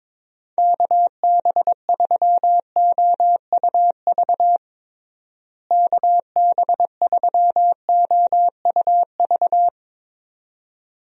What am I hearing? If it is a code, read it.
Morse code, heard as a keyed tone, about 22 wpm, 700 Hz. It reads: KB3OUV KB3OUV
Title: USA Call Signs - 6 Characters - Repeated 2x - 22wpm